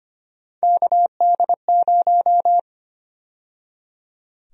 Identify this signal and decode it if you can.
Morse code, heard as a keyed tone, about 25 wpm, 700 Hz. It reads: KD0
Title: USA Call Sign Prefixes - 25wpm